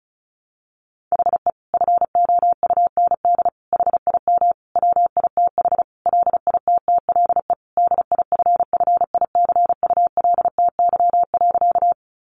Morse code, heard as a keyed tone, about 34 wpm, 700 Hz. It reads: HE FOUND HIM WITH LITTLE DIFFICULTY.
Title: Sentences from Top 1500 Words 35wpm